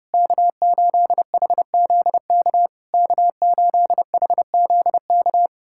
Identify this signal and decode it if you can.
Morse code, heard as a keyed tone, about 29 wpm, 690 Hz. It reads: K8HZK K8HZK
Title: SOTA QSO Practice 30wpm